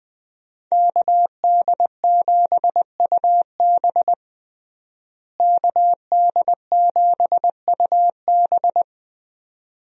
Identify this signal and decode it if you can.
Morse code, heard as a keyed tone, about 21 wpm, 700 Hz. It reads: KD7UB KD7UB
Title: USA Call Signs - 5 Characters - Repeated 2x - 20wpm